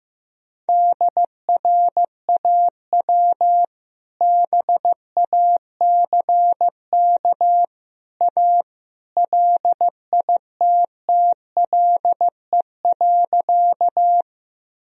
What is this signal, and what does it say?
Morse code, heard as a keyed tone, about 15 wpm, 700 Hz. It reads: DRAW BACK A LITTLE.
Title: Sentences from Top 1000 Words - Review Part 1 - 15wpm